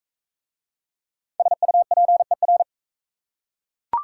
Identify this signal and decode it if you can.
Morse code, heard as a keyed tone, about 40 wpm, 700 Hz. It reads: SUPER